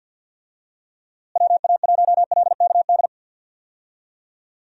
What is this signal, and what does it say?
Morse code, heard as a keyed tone, about 49 wpm, 700 Hz. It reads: WA1LKD
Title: USA Call Signs - 6 Characters - Repeated 3x - 50wpm